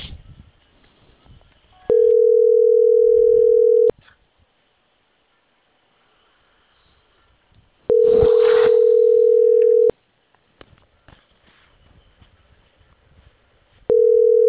Ringback tone (North America), 440 + 480 Hz.